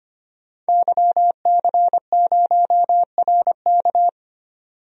Morse code, heard as a keyed tone, about 25 wpm, 700 Hz. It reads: YC0RK
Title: DXCC-100 Call Signs - 5 Characters - 25wpm